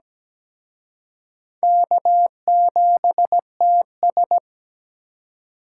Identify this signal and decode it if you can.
Morse code, heard as a keyed tone, about 20 wpm, 700 Hz. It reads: K7TS
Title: Speed Racing - USA 1x2 Call Signs 17wpm